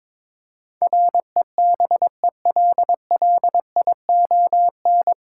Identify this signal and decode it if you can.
Morse code, heard as a keyed tone, about 22 wpm, 700 Hz. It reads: REBELLION
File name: 9 Letter Words - First 3 Letters Given - 22wpm